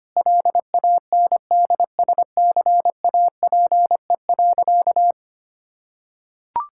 Morse code, heard as a keyed tone, about 25 wpm, 700 Hz. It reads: LANDSCAPE.
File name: practice / 25wpm / Difficult Sentences with Hints - 25wpm